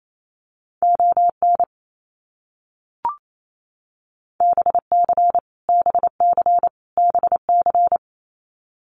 Morse code, heard as a keyed tone, about 28 wpm, 700 Hz. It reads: ON BC BC BC
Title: Canadian Provinces and Territories - Repeated 3x - 28wpm